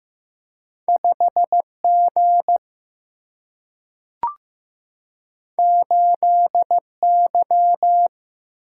Morse code, heard as a keyed tone, about 15 wpm, 700 Hz. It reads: 5G 8Y
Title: Single-Letter Single-Number 15wpm